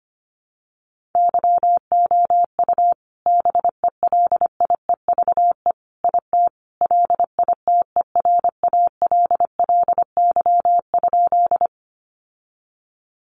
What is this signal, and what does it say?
Morse code, heard as a keyed tone, about 25 wpm, 700 Hz. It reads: YOU BELIEVE IT LITERALLY?
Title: Sentences from Top 1400 Words 25wpm